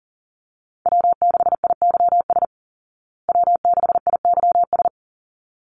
Morse code, heard as a keyed tone, about 40 wpm, 700 Hz. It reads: W6IYS W6IYS
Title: Call Signs Repeated 3x - 40wpm